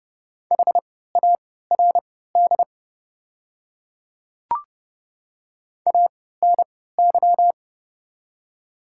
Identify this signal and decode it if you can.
Morse code, heard as a keyed tone, about 30 wpm, spaced as for 17 wpm, 700 Hz. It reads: HARD ANY
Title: 3x Character Spacing - Top 300 Words - 30wpm